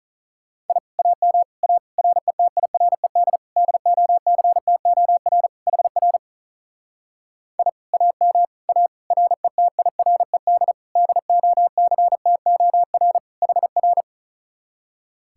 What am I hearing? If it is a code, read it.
Morse code, heard as a keyed tone, about 42 wpm, 700 Hz. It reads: I AM A RETIRED DOCTOR HR I AM A RETIRED DOCTOR HR
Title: Speed-Racing Common QSO Phrases Encore - 35wpm